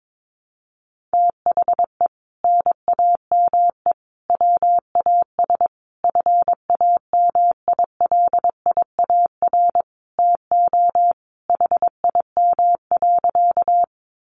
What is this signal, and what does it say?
Morse code, heard as a keyed tone, about 22 wpm, 700 Hz. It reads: THE NAME WAS FAMILIAR TO HIM.
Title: Sentences from Top 1000 Words 22wpm